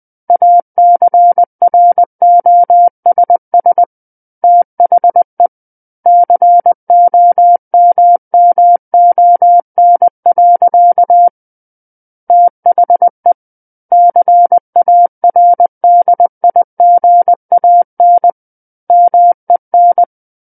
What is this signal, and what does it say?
Morse code, heard as a keyed tone, about 20 wpm, 700 Hz. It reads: ACROSS THE COMMON. THE CARDIGAN MEN
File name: 20 WPM